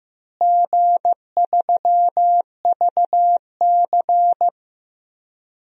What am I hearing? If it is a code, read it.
Morse code, heard as a keyed tone, about 15 wpm, 700 Hz. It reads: G3VC
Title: DXCC-100 Call Signs - 4 Characters - Repeated 2x - 15wpm